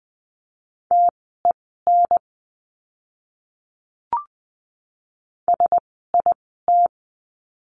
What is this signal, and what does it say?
Morse code, heard as a keyed tone, about 20 wpm, spaced as for 14 wpm, 700 Hz. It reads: TEN SIT
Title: 036 Lesson 7f - Letter s - Words 20wpm